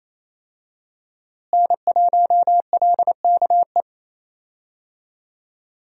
Morse code, heard as a keyed tone, about 28 wpm, 700 Hz. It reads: N1LKE
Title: USA Call Signs - 5 Characters - Repeated 2x - 28wpm